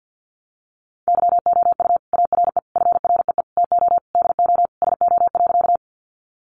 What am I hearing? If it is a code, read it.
Morse code, heard as a keyed tone, about 50 wpm, 700 Hz. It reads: YOU ARE FREE TO DO SO.
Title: Sentences from Top 500 Words 50wpm